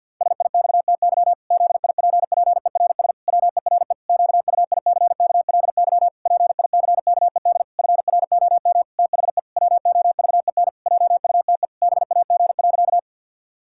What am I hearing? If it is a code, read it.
Morse code, heard as a keyed tone, about 50 wpm, 700 Hz. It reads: SIXTY ZIPPERS WERE QUICKLY PICKED FROM THE WOVEN JUTE BAG.
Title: Pangram Challenge v2 - 50wpm